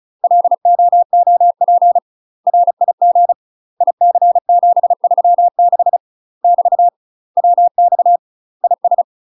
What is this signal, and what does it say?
Morse code, heard as a keyed tone, about 36 wpm, 700 Hz. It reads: LOOP RIG IC736 <BT> WX IS
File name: UK MASTER QSOs TRAINING / 35WPM / UK MASTER QSOs 0033